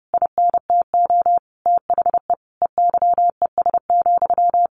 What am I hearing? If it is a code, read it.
Morse code, heard as a keyed tone, about 30 wpm, 700 Hz. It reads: INTO THE EYES,
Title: A Sound Of Thunder 30wpm